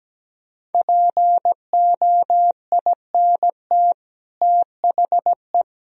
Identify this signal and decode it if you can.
Morse code, heard as a keyed tone, about 17 wpm, 700 Hz. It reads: POINT THE
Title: Sentences from Top 1000 Words - Review Part 2 - 17wpm